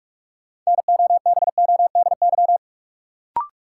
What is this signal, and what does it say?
Morse code, heard as a keyed tone, about 47 wpm, 700 Hz. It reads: NOBODY
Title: Sentences from Top 1600 Words - New Word Drill - 45wpm